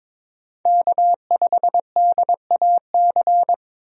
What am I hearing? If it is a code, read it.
Morse code, heard as a keyed tone, about 22 wpm, 700 Hz. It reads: K5DAC
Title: USA Call Signs - 5 Characters - 22wpm